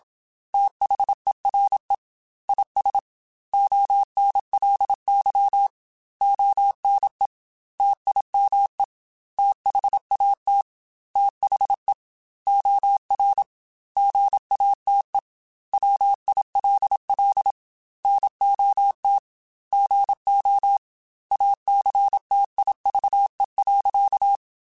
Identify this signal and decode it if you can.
Morse code, heard as a keyed tone, about 27 wpm, 800 Hz. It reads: THERE IS ONLY ONE TIME THAT THE OR GATE WILL NOT GO ACTIVE.
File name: Technical Subjects / Technical Subjects-mp3s / OR Gate @25